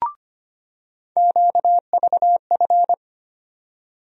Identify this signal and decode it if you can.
Morse code, heard as a keyed tone, about 25 wpm, 700 Hz. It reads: QVF
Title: USA Call Sign Postfixes - 3 Letters 25wpm